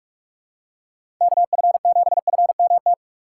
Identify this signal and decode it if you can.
Morse code, heard as a keyed tone, about 46 wpm, 700 Hz. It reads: KF7FMT
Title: USA Call Signs - 6 Characters - 6x Character Spacing - 45wpm